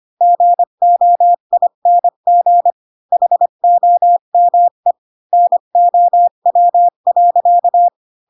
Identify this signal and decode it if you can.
Morse code, heard as a keyed tone, about 25 wpm, 700 Hz. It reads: GOING HOME NOW.
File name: Pinocchio / Pinocchio ch3 20 25 ik1hge